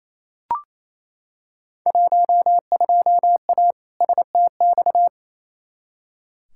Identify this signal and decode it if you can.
Morse code, heard as a keyed tone, about 27 wpm, 700 Hz. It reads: 12A STX
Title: Field Day Exchanges Encore - 28wpm